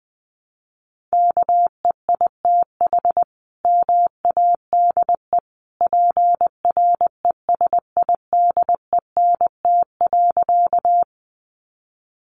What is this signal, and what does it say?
Morse code, heard as a keyed tone, about 20 wpm, 700 Hz. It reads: KEITH MADE PRESIDENT.
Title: A Sound Of Thunder 20wpm